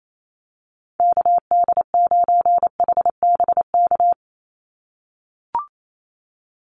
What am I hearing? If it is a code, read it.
Morse code, heard as a keyed tone, about 28 wpm, 700 Hz. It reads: KD9HBK
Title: USA Call Signs 28wpm